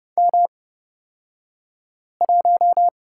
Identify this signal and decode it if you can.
Morse code, heard as a keyed tone, about 30 wpm, spaced as for 5 wpm, 700 Hz. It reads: M1
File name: Lesson 37d - Letter x - Two Characters 30wpm